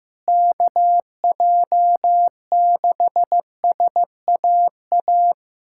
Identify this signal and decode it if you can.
Morse code, heard as a keyed tone, about 15 wpm, 700 Hz. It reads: KJ6SAA
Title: USA Callsign DE Callsign 15wpm